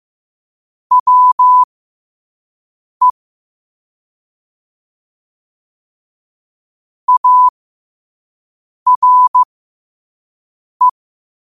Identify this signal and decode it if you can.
Morse code, heard as a keyed tone, about 15 wpm, spaced as for 5 wpm, 1000 Hz. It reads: WE ARE
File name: qotdfiles / 5wpm